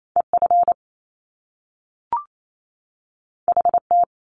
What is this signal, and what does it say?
Morse code, heard as a keyed tone, about 28 wpm, 700 Hz. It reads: EF HT
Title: Top Two Letter Combinations in Words 28wpm